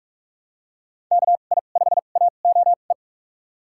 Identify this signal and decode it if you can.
Morse code, heard as a keyed tone, about 42 wpm, spaced as for 31 wpm, 700 Hz. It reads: KI5AOE
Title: USA Call Signs - 6 Characters - 2x Character Spacing - 45wpm